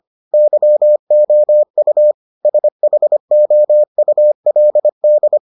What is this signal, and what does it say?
Morse code, heard as a keyed tone, about 25 wpm, 600 Hz. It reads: YOU SHOULD